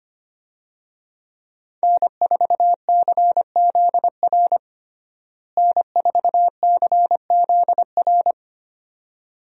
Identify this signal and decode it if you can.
Morse code, heard as a keyed tone, about 25 wpm, 700 Hz. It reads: N4CZR N4CZR
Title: USA Call Signs - 5 Characters - Repeated 2x - 25wpm